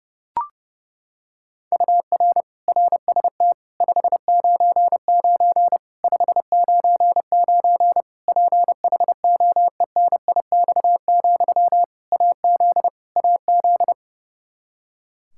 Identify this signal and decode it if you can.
Morse code, heard as a keyed tone, about 29 wpm, 700 Hz. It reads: UR RST 599 599 PHOENIX, AZ AZ
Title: K1USN SST QSO Practice 30wpm